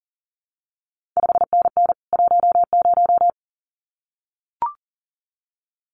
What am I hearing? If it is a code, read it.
Morse code, heard as a keyed tone, about 42 wpm, 700 Hz. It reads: 5NN 10
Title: CQ WW DX Contest Exchange Encore - 40wpm